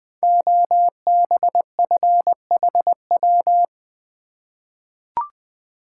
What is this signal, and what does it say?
Morse code, heard as a keyed tone, about 20 wpm, 700 Hz. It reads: OBFHW
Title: Code Groups - 5 Letters - 20wpm